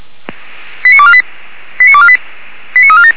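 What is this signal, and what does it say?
Five-tone selective call (ZVEI), heard as quick runs of five tones, tones of about 70 ms. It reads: 89248 89248 89347
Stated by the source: ZVEI five-tone call received by SDR.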